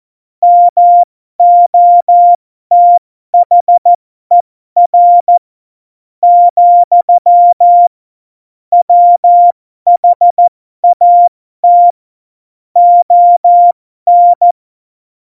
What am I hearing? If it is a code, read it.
Morse code, heard as a keyed tone, about 14 wpm, spaced as for 11 wpm, 700 Hz. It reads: MOTHER, WHAT ON
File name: Morsekurs / Witze